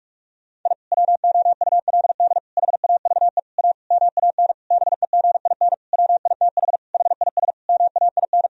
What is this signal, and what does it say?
Morse code, heard as a keyed tone, about 41 wpm, 700 Hz. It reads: I WOULD HAVE A MAN BEGIN WITH HIS MAIN
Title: Sentences from Top 1000 Words - Review Part 2 - 45wpm